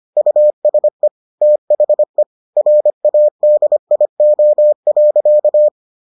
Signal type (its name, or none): Morse code